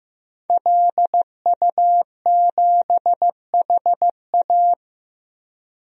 Morse code, heard as a keyed tone, about 15 wpm, 700 Hz. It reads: LU7HA
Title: DXCC-100 Call Signs - 5 Characters - 15wpm